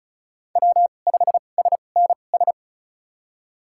Morse code, heard as a keyed tone, about 34 wpm, spaced as for 25 wpm, 700 Hz. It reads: W5SNS
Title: USA Call Signs - 5 Characters - 2x Character Spacing - 35wpm